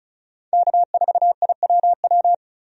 Morse code, heard as a keyed tone, about 36 wpm, 700 Hz. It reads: K4IWW
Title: USA Call Signs - 5 Characters - 35wpm